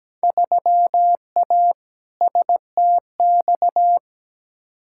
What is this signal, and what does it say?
Morse code, heard as a keyed tone, about 17 wpm, 700 Hz. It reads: 3A STX
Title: Field Day Exchanges - 17wpm